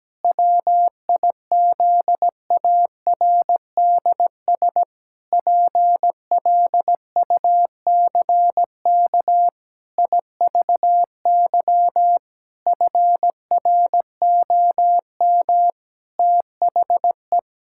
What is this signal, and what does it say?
Morse code, heard as a keyed tone, about 17 wpm, 700 Hz. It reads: WIZARDS PLUCK IVY FROM THE
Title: Pangram Challenge v2 - 17wpm